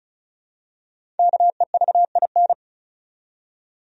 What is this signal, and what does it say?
Morse code, heard as a keyed tone, about 36 wpm, 700 Hz. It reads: KEVIN
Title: USA Common Names - Max 5 Letters - 35wpm